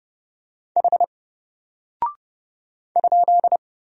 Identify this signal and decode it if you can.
Morse code, heard as a keyed tone, about 30 wpm, 700 Hz. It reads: H ?
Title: Single Character (Letter:Number:Punctuation:Prosign) Alt 30wpm